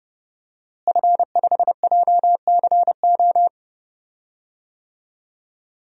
Morse code, heard as a keyed tone, about 30 wpm, 700 Hz. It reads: F5JCO
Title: DXCC-100 Call Signs - 5 Characters - Repeated 3x - 30wpm